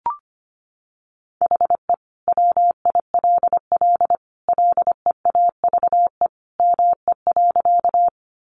Morse code, heard as a keyed tone, about 25 wpm, 700 Hz. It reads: HE WILL LEAVE ME.
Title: Sentences from Top 400 Words 25wpm